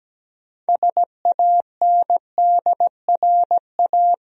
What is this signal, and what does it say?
Morse code, heard as a keyed tone, about 17 wpm, 700 Hz. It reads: SANDRA
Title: USA Common Names - 17wpm